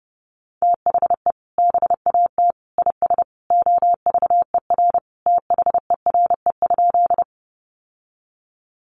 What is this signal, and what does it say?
Morse code, heard as a keyed tone, about 30 wpm, 700 Hz. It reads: THE BAT IS OVER THERE?
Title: Sentences from Top 1500 Words 30wpm